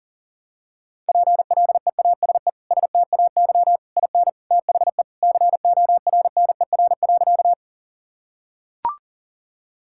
Morse code, heard as a keyed tone, about 40 wpm, 700 Hz. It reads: PLEASE STAY IN THE CORNER.
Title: Sentences from Top 1600 Words 40wpm